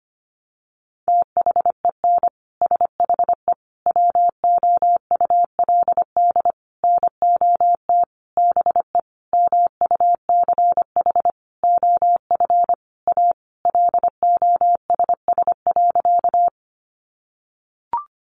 Morse code, heard as a keyed tone, about 25 wpm, 700 Hz. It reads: THEN SHE WOULD NOT BE MUCH OF A LOSS.
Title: Sentences from Top 1400 Words 25wpm